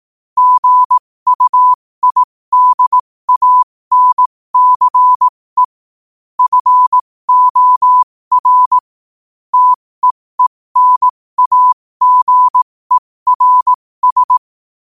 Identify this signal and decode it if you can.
Morse code, heard as a keyed tone, about 18 wpm, spaced as for 15 wpm, 1000 Hz. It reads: GUIDANCE FOR TEENAGERS
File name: qotdfiles / 15wpm